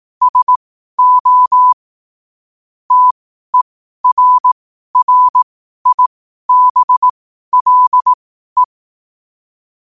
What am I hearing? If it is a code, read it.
Morse code, heard as a keyed tone, about 18 wpm, spaced as for 12 wpm, 1000 Hz. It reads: SO TERRIBLE
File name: qotdfiles / 12wpm